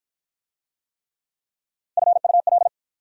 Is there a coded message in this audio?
Yes: Morse code from 2.0 to 2.7 s, heard as a keyed tone, about 44 wpm, 690 Hz. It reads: FUL